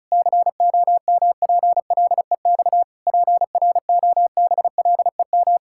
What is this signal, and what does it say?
Morse code, heard as a keyed tone, about 36 wpm, 700 Hz. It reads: COMPLEX PROBLEM
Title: Sentences from Top 1000 Words - Review Part 6 - 35wpm